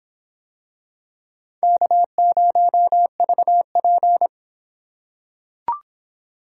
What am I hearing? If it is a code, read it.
Morse code, heard as a keyed tone, about 25 wpm, 700 Hz. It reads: K0VP